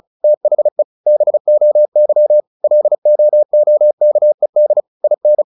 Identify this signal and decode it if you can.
Morse code, heard as a keyed tone, about 35 wpm, 600 Hz. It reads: THE BOY LOOKED IN
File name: the old man and the sea 2 35wpm